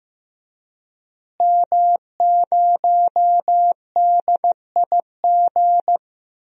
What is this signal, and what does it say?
Morse code, heard as a keyed tone, about 14 wpm, 700 Hz. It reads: M0DIG
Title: DXCC-100 Call Signs - 5 Characters - Repeated 3x - 15wpm